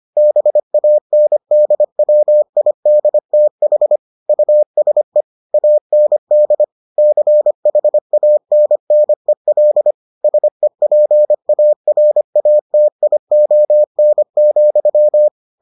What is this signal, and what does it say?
Morse code, heard as a keyed tone, about 25 wpm, 600 Hz. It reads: BANDWIDTH USE AND CHANNEL SEPARATION,